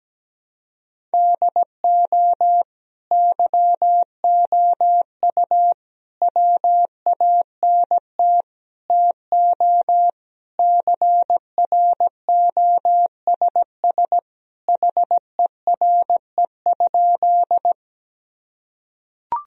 Morse code, heard as a keyed tone, about 17 wpm, 700 Hz. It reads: DO YOU WANT TO CROSS HERE?
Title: Sentences from Top 1000 Words - Review Part 1 - 17wpm